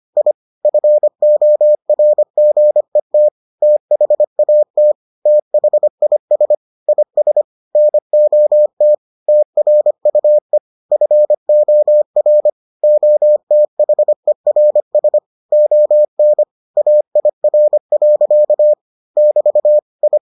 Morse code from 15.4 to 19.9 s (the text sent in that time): ON AIR. <BT>